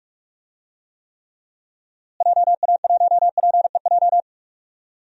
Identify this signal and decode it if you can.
Morse code, heard as a keyed tone, about 46 wpm, 700 Hz. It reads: JA1PEJ
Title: DXCC-100 Call Signs - 6 Characters - Repeated 3x - 45wpm